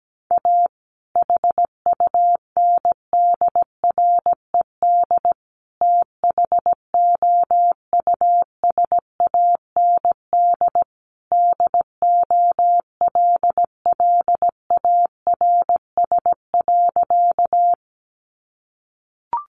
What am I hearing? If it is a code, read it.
Morse code, heard as a keyed tone, about 17 wpm, 700 Hz. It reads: A HUNDRED THOUSAND DOLLARS.
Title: A Sound Of Thunder 17wpm